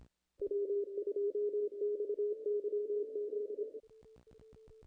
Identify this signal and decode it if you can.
Morse code, heard as a keyed tone, about 26 wpm, 440 Hz. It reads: W2XYZ
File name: contests / ss-cw-k5tr-op-n5rz -0000